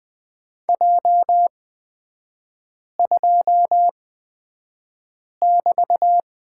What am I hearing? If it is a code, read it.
Morse code, heard as a keyed tone, about 20 wpm, spaced as for 5 wpm, 700 Hz. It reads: J2<BT>
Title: 224 Lesson 36e - BT - Three Characters 20wpm